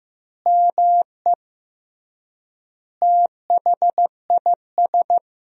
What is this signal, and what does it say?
Morse code, heard as a keyed tone, about 15 wpm, 700 Hz. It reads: ME THIS
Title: 3x Word-Spacing - Sentences from Top 500 Words - 15wpm